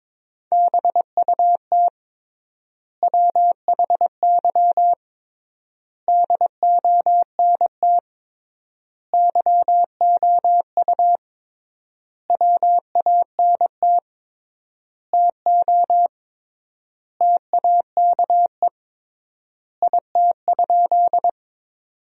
Morse code, heard as a keyed tone, about 21 wpm, 700 Hz. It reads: BUT WHY DONT YOU WANT TO TAKE IT?
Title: 3x Word-Spacing - Sentences from Top 300 Words - 22wpm